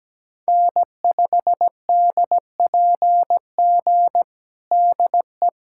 Morse code, heard as a keyed tone, about 17 wpm, 700 Hz. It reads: N5DPG DE
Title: USA Callsign DE Callsign - 5 Characters - 17wpm